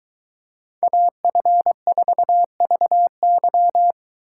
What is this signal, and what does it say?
Morse code, heard as a keyed tone, about 22 wpm, 700 Hz. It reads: AF4VY